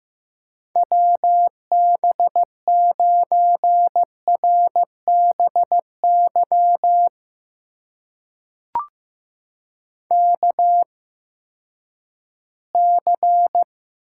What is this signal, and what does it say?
Morse code, heard as a keyed tone, about 15 wpm, spaced as for 10 wpm, 700 Hz. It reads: WB9RBY KC
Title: USA Call Signs - 6 Characters - 8x Character Spacing - 15wpm